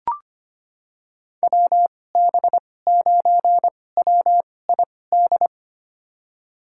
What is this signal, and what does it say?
Morse code, heard as a keyed tone, about 25 wpm, spaced as for 18 wpm, 700 Hz. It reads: WB9WID
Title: USA Call Signs - 6 Characters - 2x Character Spacing - 25wpm